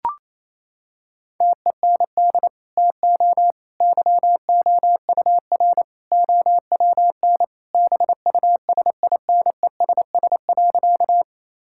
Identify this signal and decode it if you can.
Morse code, heard as a keyed tone, about 27 wpm, 700 Hz. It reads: TEND TO YOUR OWN BUSINESS.
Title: Sentences from Top 1000 Words - Review Part 5 - 28wpm